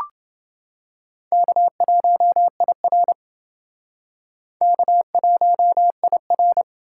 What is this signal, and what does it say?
Morse code, heard as a keyed tone, about 28 wpm, 700 Hz. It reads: K1IR K1IR